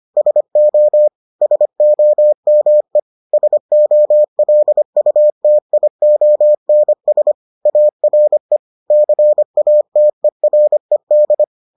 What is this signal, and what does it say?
Morse code, heard as a keyed tone, about 25 wpm, 600 Hz. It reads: SO SOME SOLUTIONS ARE CATERED